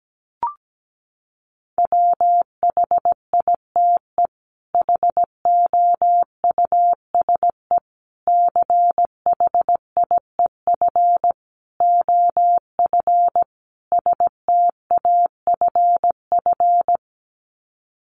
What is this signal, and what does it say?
Morse code, heard as a keyed tone, about 17 wpm, 700 Hz. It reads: WHITE HOUSE CHIEF OF STAFF
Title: Sets of 5 Words Encore 17wpm